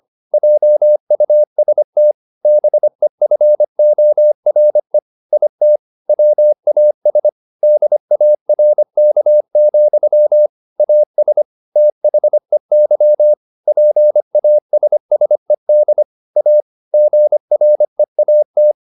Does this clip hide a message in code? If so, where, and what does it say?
Yes: Morse code from 0.3 to 18.7 s, heard as a keyed tone, about 25 wpm, 600 Hz. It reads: JUST BEFORE IT WAS DARK, AS THEY PASSED A GREAT